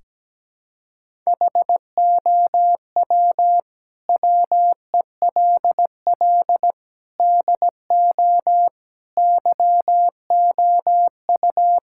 Morse code, heard as a keyed tone, about 17 wpm, 700 Hz. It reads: HOW WELL DO YOU
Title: Sentences from Top 1000 Words - Review Part 5 - 17wpm